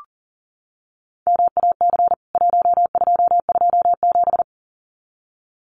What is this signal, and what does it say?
Morse code, heard as a keyed tone, about 40 wpm, 700 Hz. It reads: MAC 1227
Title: CWT Exchange 40wpm